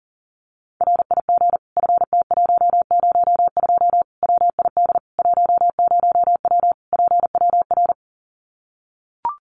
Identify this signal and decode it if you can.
Morse code, heard as a keyed tone, about 37 wpm, 700 Hz. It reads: RIG FT102 WID 10W PWR
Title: Common QSO Phrases 40wpm